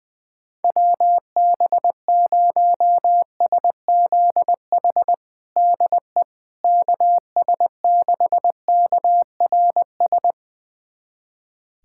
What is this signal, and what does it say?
Morse code, heard as a keyed tone, about 20 wpm, 700 Hz. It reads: WB0SZH DE KS6KRS
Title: USA Callsign DE Callsign 20wpm